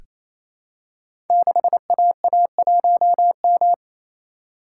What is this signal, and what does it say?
Morse code, heard as a keyed tone, about 29 wpm, 700 Hz. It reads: 6AA1M
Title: Code Groups - 5 Letter:Numbers - 28wpm